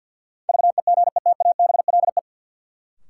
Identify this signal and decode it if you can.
Morse code, heard as a keyed tone, about 53 wpm, 700 Hz. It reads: VEGETABLE
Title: Sentences from Top 1300 Words - New Word Drill Alt - 50wpm